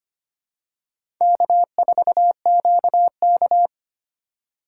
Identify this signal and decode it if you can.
Morse code, heard as a keyed tone, about 25 wpm, 700 Hz. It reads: K4QK
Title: USA Call Signs - 4 Characters - 2x Character Spacing - 25wpm